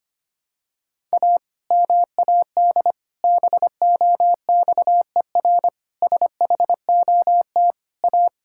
Morse code, heard as a keyed tone, about 25 wpm, 700 Hz. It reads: A MAD BOXER SHOT A
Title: Pangram Challenge v2 - 25wpm